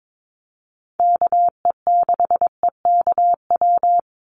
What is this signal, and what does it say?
Morse code, heard as a keyed tone, about 22 wpm, 700 Hz. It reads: KE6EKW
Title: USA Call Signs 22wpm